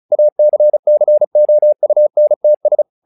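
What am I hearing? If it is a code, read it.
Morse code, heard as a keyed tone, about 36 wpm, 600 Hz. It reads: ACCOUNTS
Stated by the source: Morse practice text "Tower Trouble" sent at 35 WPM, from the SKCC - Straight Key Century Club learning center audio files.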